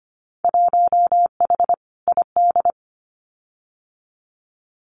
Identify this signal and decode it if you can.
Morse code, heard as a keyed tone, about 25 wpm, 690 Hz. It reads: 1H ID
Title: Winter Field Day Exchanges 25wpm